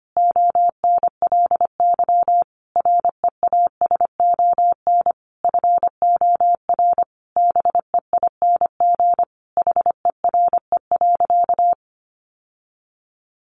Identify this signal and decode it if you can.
Morse code, heard as a keyed tone, about 25 wpm, 700 Hz. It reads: ONLY REASON FOR BEING HERE.
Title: Sentences from Top 700 Words 25wpm